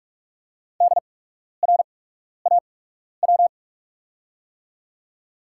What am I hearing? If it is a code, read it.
Morse code, heard as a keyed tone, about 48 wpm, spaced as for 12 wpm, 690 Hz. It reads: DRAW
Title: 8x Character Spacing - Top 300 Words - 45wpm